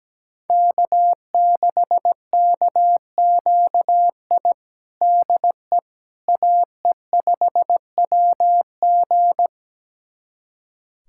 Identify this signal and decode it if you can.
Morse code, heard as a keyed tone, about 17 wpm, 700 Hz. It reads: K6KQI DE AE5WG
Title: USA Callsign DE Callsign - 5 Characters - 17wpm